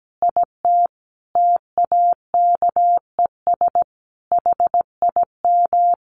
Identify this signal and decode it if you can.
Morse code, heard as a keyed tone, about 17 wpm, 700 Hz. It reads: IT TAKES HIM
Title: Additional Easy Phrases 17wpm